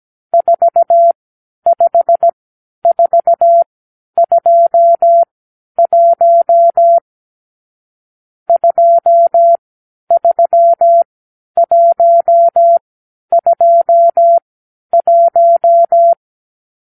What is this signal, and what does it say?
Morse code, heard as a keyed tone, about 17 wpm, spaced as for 10 wpm, 680 Hz. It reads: 45421 23121
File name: arrl / Review (1-5)